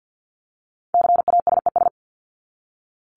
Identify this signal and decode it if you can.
Morse code, heard as a keyed tone, about 50 wpm, 700 Hz. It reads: CASES
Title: ES:S Suffix - Top 100 Words with Base Words 50wpm